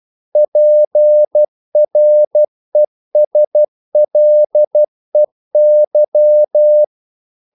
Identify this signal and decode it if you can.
Morse code, heard as a keyed tone, about 12 wpm, 600 Hz. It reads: PRESLEY
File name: CW / Practice / EnglishQuotations0312